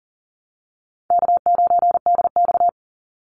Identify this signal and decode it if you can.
Morse code, heard as a keyed tone, about 40 wpm, 700 Hz. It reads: K9DX
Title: Easy Call Signs Repeated 2x - 40wpm